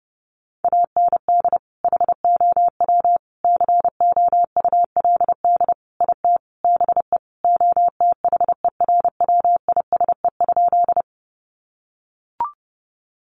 Morse code, heard as a keyed tone, about 30 wpm, 700 Hz. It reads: AND HOW COULD IT BE OTHERWISE?
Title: Sentences from Top 1200 Words 30wpm